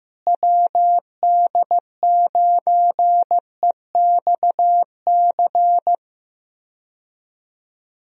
Morse code, heard as a keyed tone, about 15 wpm, 700 Hz. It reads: WD9EXC
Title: USA Call Signs - 6 Characters - Repeated 3x - 15wpm